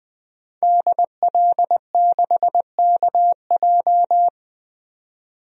Morse code, heard as a keyed tone, about 20 wpm, 700 Hz. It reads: DL6KJ
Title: DXCC-100 Call Signs - 5 Characters - 20wpm